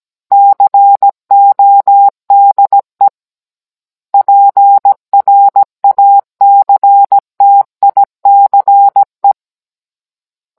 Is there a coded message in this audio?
Yes: Morse code from 0.3 to 9.3 s, heard as a keyed tone, about 17 wpm, 800 Hz. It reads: CODE PRACTICE